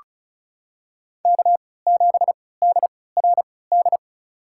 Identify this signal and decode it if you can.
Morse code, heard as a keyed tone, about 36 wpm, spaced as for 20 wpm, 700 Hz. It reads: K7DRD
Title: USA Call Signs - 5 Characters - 3x Character Spacing - 35wpm